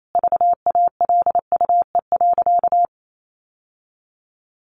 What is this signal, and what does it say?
Morse code, heard as a keyed tone, about 28 wpm, 700 Hz. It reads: VALUE.
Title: Sentences from Top 1400 Words 28wpm